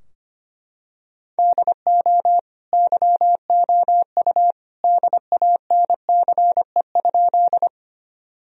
Morse code, heard as a keyed tone, about 25 wpm, 700 Hz. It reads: DO YOU DANCE?
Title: Sentences from Top 1000 Words - Review Part 4 - 25wpm